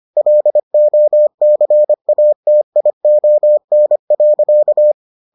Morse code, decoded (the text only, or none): LOCATION.